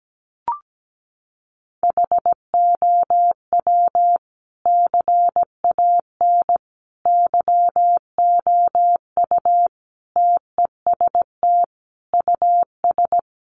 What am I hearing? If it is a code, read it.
Morse code, heard as a keyed tone, about 17 wpm, 700 Hz. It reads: HOW CAN YOU TEST US
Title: Sentences from Top 1100 Words 17wpm